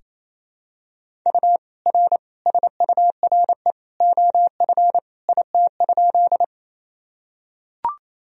Morse code, heard as a keyed tone, about 28 wpm, 700 Hz. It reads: U R SURE OF IT?
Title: Sentences from Top 300 Words and Abbreviations 28wpm